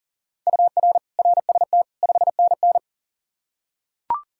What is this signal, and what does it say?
Morse code, heard as a keyed tone, about 39 wpm, 700 Hz. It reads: UR RST 5NN
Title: K1USN SST QSO Practice 40wpm